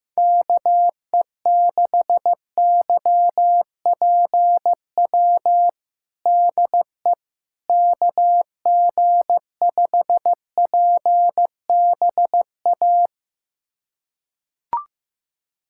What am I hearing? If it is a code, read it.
Morse code, heard as a keyed tone, about 16 wpm, 700 Hz. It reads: KE6YPW DE KG5PBA
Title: USA Callsign DE Callsign - 6 Characters - 15wpm